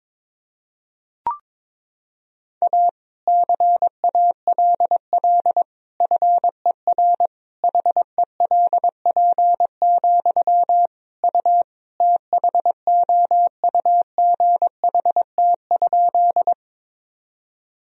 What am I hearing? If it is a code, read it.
Morse code, heard as a keyed tone, about 22 wpm, 700 Hz. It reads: A CALL FER HELP, U THOUGHT?
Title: Sentences from Top 300 Words and Abbreviations 22wpm